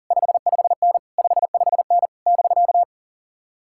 Morse code, heard as a keyed tone, about 49 wpm, 700 Hz. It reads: 55N 55N <BK>
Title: Speed-Racing Common QSO Phrases Encore - 40wpm